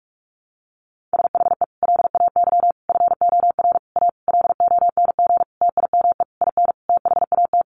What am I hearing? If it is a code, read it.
Morse code, heard as a keyed tone, about 44 wpm, 700 Hz. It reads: SHE LAY FOR A LONG TIME IN THAT
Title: Sentences from Top 900 Words 45wpm